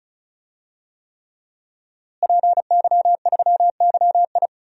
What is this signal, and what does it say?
Morse code, heard as a keyed tone, about 36 wpm, 700 Hz. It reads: PY3YI
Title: DXCC-100 Call Signs - 5 Characters - Repeated 3x - 35wpm